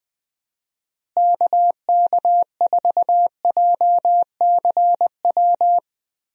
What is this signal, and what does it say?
Morse code, heard as a keyed tone, about 21 wpm, 700 Hz. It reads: KK4JCW
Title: USA Call Signs - 6 Characters - Repeated 2x - 20wpm